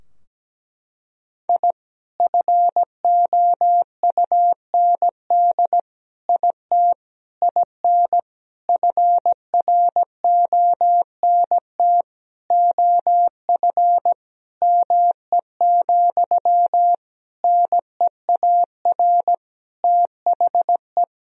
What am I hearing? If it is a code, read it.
Morse code, heard as a keyed tone, about 17 wpm, 700 Hz. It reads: I FOUND IT IN FRONT OF ME, NEAR THE
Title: Sentences from Top 1000 Words - Review Part 2 - 17wpm